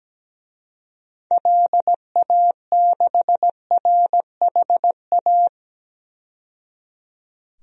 Morse code, heard as a keyed tone, about 17 wpm, 700 Hz. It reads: LA6RHA
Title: DXCC-100 Call Signs - 6 Characters - 17wpm